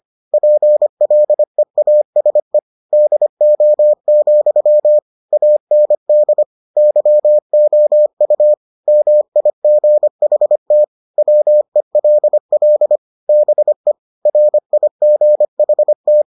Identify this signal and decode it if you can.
Morse code, heard as a keyed tone, about 25 wpm, 600 Hz. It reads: PLEASE DO, AND YOU MIGHT WELL BE RIGHT